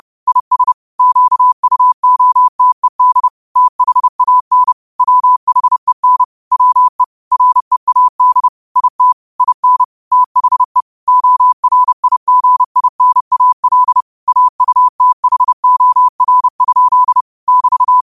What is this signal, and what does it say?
Morse code, heard as a keyed tone, about 30 wpm, 1000 Hz. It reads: IS QUOTED THAN WHEN WE READ IT IN THE ORIGINAL AUTHOR? <BT>
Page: Quotes for Wed, 13 Aug 2025 in Morse Code at 30 words per minute.